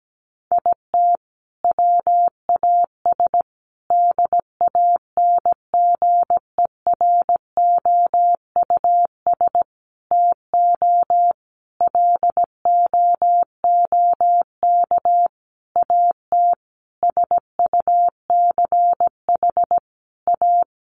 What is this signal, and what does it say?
Morse code, heard as a keyed tone, about 17 wpm, 700 Hz. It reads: IT WAS DANGEROUS TO LOOK AT SUCH A
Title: Sentences from Top 1300 Words 17wpm